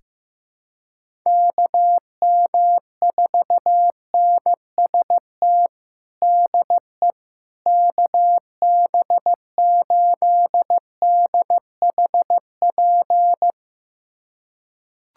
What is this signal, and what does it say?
Morse code, heard as a keyed tone, about 15 wpm, 700 Hz. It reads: KM4NST DE KB8DHP
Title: USA Callsign DE Callsign 15wpm